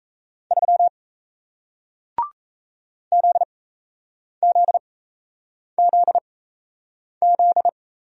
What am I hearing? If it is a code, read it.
Morse code, heard as a keyed tone, about 38 wpm, spaced as for 8 wpm, 700 Hz. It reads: 3 ZZZZ